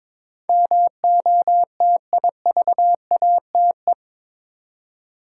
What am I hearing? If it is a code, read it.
Morse code, heard as a keyed tone, about 22 wpm, 700 Hz. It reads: MOTIVATE
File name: practice / 22wpm / Eight Letter Words - 22wpm